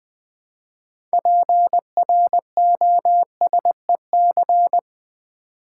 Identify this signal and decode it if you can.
Morse code, heard as a keyed tone, about 20 wpm, 700 Hz. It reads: PROSEC
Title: Build A Word - 3 Letters At A Time - 9-Letter Words - 20wpm